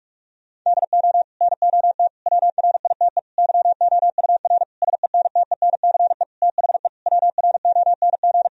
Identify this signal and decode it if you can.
Morse code, heard as a keyed tone, about 43 wpm, 700 Hz. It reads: DO NOT WRITE YOUR SENTENCE THE WRONG
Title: Sentences from Top 1000 Words - Review Part 1 - 45wpm